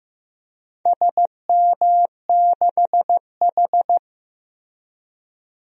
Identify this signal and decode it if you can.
Morse code, heard as a keyed tone, about 15 wpm, 700 Hz. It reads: SM6H
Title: DXCC-100 Call Signs - 4 Characters - Repeated 2x - 15wpm